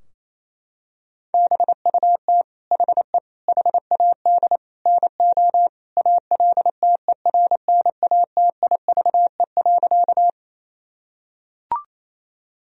Morse code, heard as a keyed tone, about 28 wpm, 700 Hz. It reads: BUT HE HAD NO ALTERNATIVE.
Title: Sentences from Top 1000 Words - Review Part 4 - 28wpm